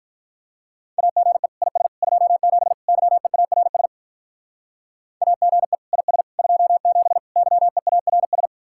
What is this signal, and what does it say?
Morse code, heard as a keyed tone, about 47 wpm, 700 Hz. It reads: AGE IS 27 YEARS AGE IS 27 YEARS